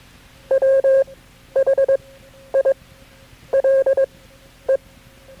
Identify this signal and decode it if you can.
Morse code, heard as a keyed tone, about 21 wpm, spaced as for 10 wpm, 540 Hz. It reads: WHILE
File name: KB6MT-tape3-side1